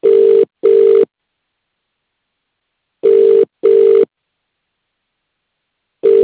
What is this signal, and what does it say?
Ringback tone (UK), 400 + 450 Hz.